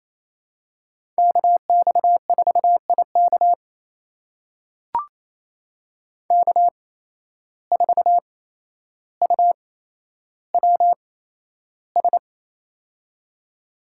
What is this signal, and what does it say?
Morse code, heard as a keyed tone, about 28 wpm, spaced as for 12 wpm, 700 Hz. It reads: KX4IK K4UWS
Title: USA Call Signs - 5 Characters - 8x Character Spacing - 28wpm